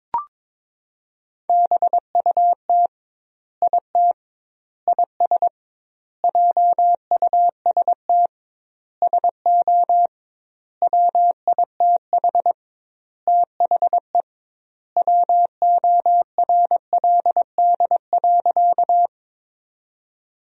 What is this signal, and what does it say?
Morse code, heard as a keyed tone, about 22 wpm, 700 Hz. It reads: BUT IT IS JUST SO WITH THE WORLD.
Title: 2x Word-Spacing - Sentences from Top 200 Words - 22wpm